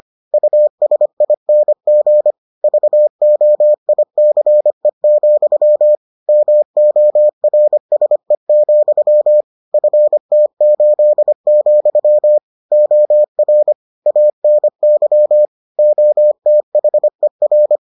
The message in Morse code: USING VOICE, MORSE, FT8, OR ANY OTHER

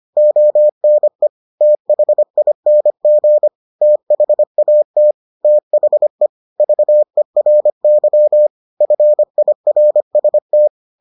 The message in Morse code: ONE THING THAT THE VERY FIRST

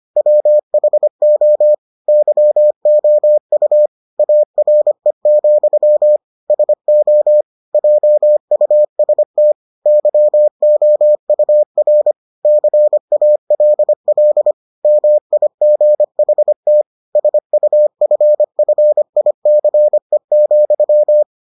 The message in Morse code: WHO YOU ARE, SO JUST YOUR CALL MIGHT SUFFICE,

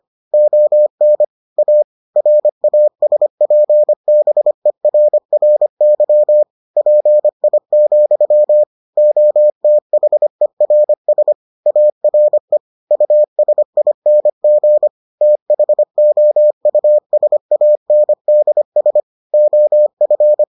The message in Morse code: ON A RASPBERRY PI, OTHERS ARE USING THOUSANDS OF

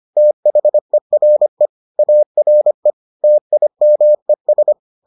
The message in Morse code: THERE ARE TIMES

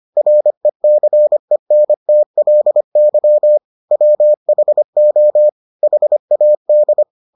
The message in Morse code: RECENTLY WHO HAD

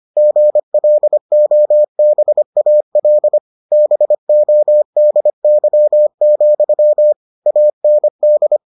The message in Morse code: GLOBAL BODY, AND

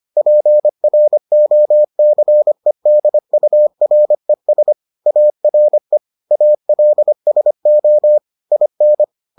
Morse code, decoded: PROCEDURES ARE ALSO IN

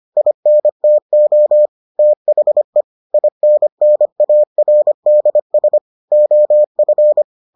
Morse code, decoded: INTO THE INNARDS OF

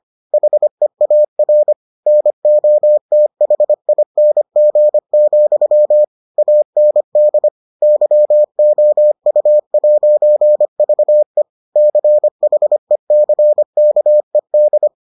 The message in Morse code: HEAR NOTHING, AND YOU'VE CHECKED

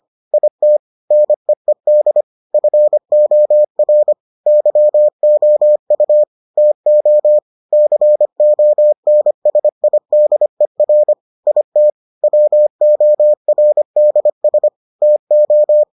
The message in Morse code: IT NEED FOR YOU TO CONSIDER IT WORDS TO

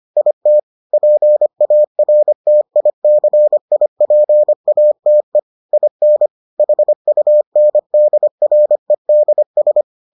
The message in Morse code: IT PARTICIPATE IN HUNDREDS

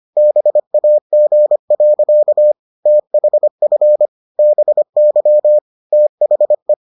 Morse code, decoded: BAG. THF BY THE

